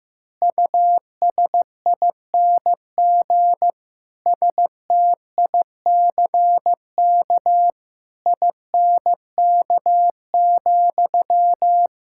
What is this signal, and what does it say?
Morse code, heard as a keyed tone, about 15 wpm, 700 Hz. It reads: USING STICK INK,
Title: Pangram Challenge v2 - 15wpm